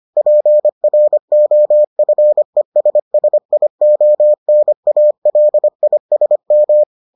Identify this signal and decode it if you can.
Morse code, heard as a keyed tone, about 25 wpm, 600 Hz. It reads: PROFESSIONALISM